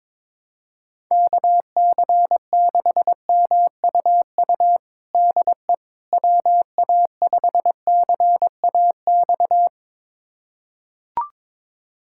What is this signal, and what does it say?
Morse code, heard as a keyed tone, about 22 wpm, 700 Hz. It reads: KC6MUU DE WA5CAX
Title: USA Callsign DE Callsign - 6 Characters - 22wpm